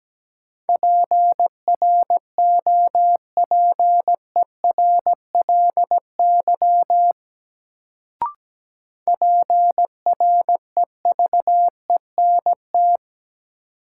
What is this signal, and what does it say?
Morse code, heard as a keyed tone, about 17 wpm, 700 Hz. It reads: PROPERLY PREVENT
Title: Sentences from Top 1100 Words - New Word Drill - 17wpm